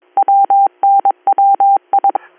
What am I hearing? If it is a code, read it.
Morse code, heard as a keyed tone, about 22 wpm, 790 Hz. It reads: WNWI
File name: 310- .0-call 5362